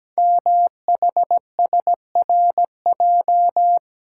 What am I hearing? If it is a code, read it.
Morse code, heard as a keyed tone, about 17 wpm, 700 Hz. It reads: MHSRJ
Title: Code Groups - 5 Letters - 17wpm